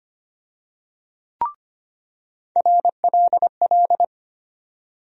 Morse code, heard as a keyed tone, about 25 wpm, 700 Hz. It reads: RLL
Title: USA Call Sign Postfixes - 3 Letters 25wpm